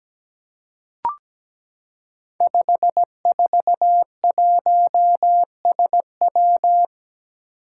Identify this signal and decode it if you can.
Morse code, heard as a keyed tone, about 17 wpm, 700 Hz. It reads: 541SW
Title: Five Challenging Letter-Number Combinations 17wpm